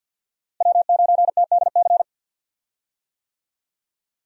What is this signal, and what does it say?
Morse code, heard as a keyed tone, about 52 wpm, 700 Hz. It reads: W9TDC
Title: USA Call Signs - 5 Characters - Repeated 2x - 50wpm